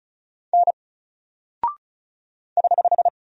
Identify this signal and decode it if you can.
Morse code, heard as a keyed tone, about 36 wpm, 700 Hz. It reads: N <HH>
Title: Single Character (Letter:Number:Punctuation:Prosign) 35wpm